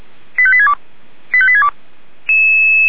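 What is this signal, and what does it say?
Five-tone selective call (ZVEI), heard as quick runs of five tones, tones of about 70 ms. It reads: 85862 85862
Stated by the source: ZVEI five-tone selective call 85861 received by SDR.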